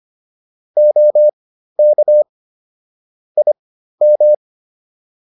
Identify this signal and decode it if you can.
Morse code, heard as a keyed tone, about 25 wpm, spaced as for 13 wpm, 600 Hz. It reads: OK IM